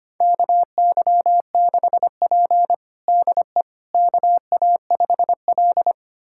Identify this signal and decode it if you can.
Morse code, heard as a keyed tone, about 25 wpm, 700 Hz. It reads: KY6P DE KA5L
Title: USA Callsign DE Callsign - 4 Characters - 25wpm